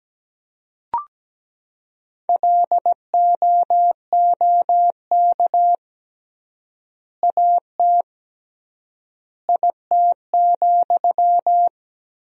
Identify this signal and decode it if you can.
Morse code, heard as a keyed tone, about 17 wpm, 700 Hz. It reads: LOOK AT IT,
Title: 3x Word-Spacing - Sentences from Top 200 Words - 17wpm